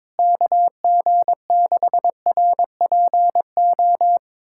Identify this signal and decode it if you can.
Morse code, heard as a keyed tone, about 22 wpm, 700 Hz. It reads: KG6RPO
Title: USA Callsign DE Callsign 22wpm